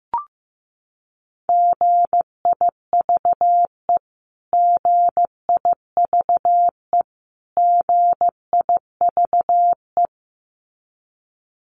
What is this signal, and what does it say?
Morse code, heard as a keyed tone, about 15 wpm, 700 Hz. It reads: GIVE GIVE GIVE
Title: Top 200 Words Repeated 15wpm